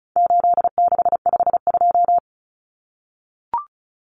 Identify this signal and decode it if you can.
Morse code, heard as a keyed tone, about 34 wpm, 700 Hz. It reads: 8652
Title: Common Cut Numbers - 4-Digit 35wpm